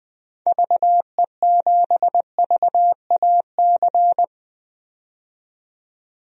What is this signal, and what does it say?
Morse code, heard as a keyed tone, about 20 wpm, 700 Hz. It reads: VE7VAC
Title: DXCC-100 Call Signs - 6 Characters - 20wpm